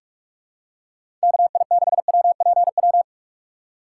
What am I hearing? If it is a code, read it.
Morse code, heard as a keyed tone, about 47 wpm, 700 Hz. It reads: KI6WPW
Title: USA Call Signs - 6 Characters - 3x Character Spacing - 45wpm